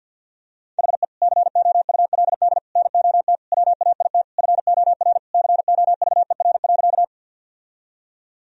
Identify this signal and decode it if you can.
Morse code, heard as a keyed tone, about 47 wpm, 700 Hz. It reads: HE COULD NOT WAIT FOR COVER.
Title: Sentences from Top 1000 Words - Review Part 1 - 50wpm